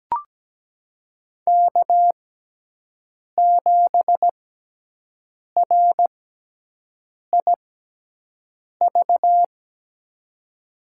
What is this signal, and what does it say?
Morse code, heard as a keyed tone, about 17 wpm, spaced as for 6 wpm, 700 Hz. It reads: K7RIV
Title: USA Call Signs - 5 Characters - 6x Character Spacing - 17wpm